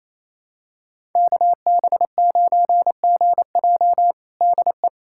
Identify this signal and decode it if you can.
Morse code, heard as a keyed tone, about 28 wpm, 700 Hz. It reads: KB9GJ DE
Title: USA Callsign DE Callsign - 5 Characters - 28wpm